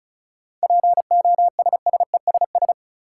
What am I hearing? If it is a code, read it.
Morse code, heard as a keyed tone, about 36 wpm, 700 Hz. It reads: POSSESS
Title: Seven Letter Words - 35wpm